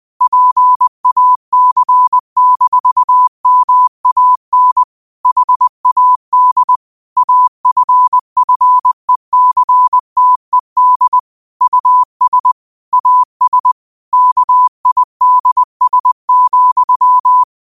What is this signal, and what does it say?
Morse code, heard as a keyed tone, about 20 wpm, 1000 Hz. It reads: PAC-MAN HAD AFFECTED US AS KIDS,
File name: qotdfiles / 20wpm